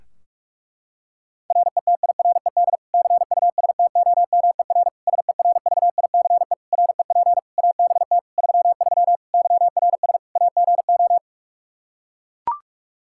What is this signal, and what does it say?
Morse code, heard as a keyed tone, about 41 wpm, 700 Hz. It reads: RETIRED CUSTOMER SERVICE REP ABT 33 YRS AGO
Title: Practice QSO - Exchange 3 - More Info - 45wpm